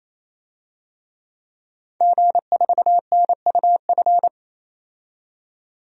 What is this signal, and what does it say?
Morse code, heard as a keyed tone, about 28 wpm, 700 Hz. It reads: G4NUF
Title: DXCC-100 Call Signs - 5 Characters - 28wpm